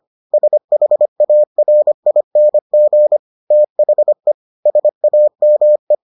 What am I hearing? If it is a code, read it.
Morse code, heard as a keyed tone, about 25 wpm, 600 Hz. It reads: SHARING THE SAME